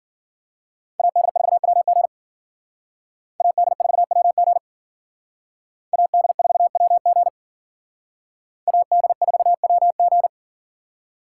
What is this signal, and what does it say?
Morse code, heard as a keyed tone, about 48 wpm, 700 Hz. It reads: AD4WG AD4WG AD4WG AD4WG
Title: Speed Racing - USA 2x2 Call Signs 40wpm